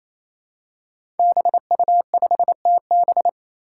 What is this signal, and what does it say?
Morse code, heard as a keyed tone, about 27 wpm, 700 Hz. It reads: BU5TB
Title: Five Challenging Letter-Number Combinations 28wpm